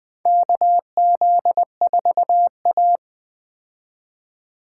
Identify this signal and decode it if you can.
Morse code, heard as a keyed tone, about 20 wpm, 700 Hz. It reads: KZ4A
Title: K1USN SST QSO Practice - 20wpm